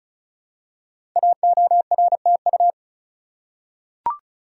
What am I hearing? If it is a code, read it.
Morse code, heard as a keyed tone, about 36 wpm, 690 Hz. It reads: AORTU
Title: Code Groups - 5 Letters - 35wpm